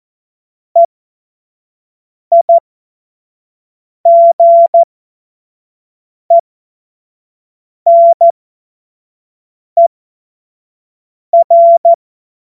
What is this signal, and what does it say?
Morse code, heard as a keyed tone, about 14 wpm, spaced as for 5 wpm, 680 Hz. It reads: EIGENER
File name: Morsekurs / Witze / Witz 31. Januar 2023 25 F70